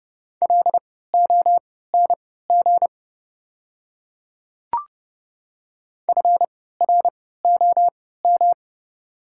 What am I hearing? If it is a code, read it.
Morse code, heard as a keyed tone, about 30 wpm, spaced as for 17 wpm, 700 Hz. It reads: LONG FROM
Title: 3x Character Spacing - Top 100 Words - 30wpm